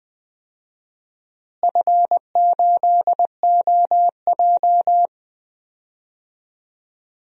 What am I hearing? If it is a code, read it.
Morse code, heard as a keyed tone, about 20 wpm, 700 Hz. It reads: F8OJ
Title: DXCC-100 Call Signs - 4 Characters - Repeated 3x - 20wpm